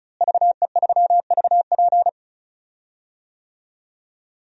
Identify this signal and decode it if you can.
Morse code, heard as a keyed tone, about 36 wpm, 700 Hz. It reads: VE3VP
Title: DXCC-100 Call Signs - 5 Characters - Repeated 2x - 35wpm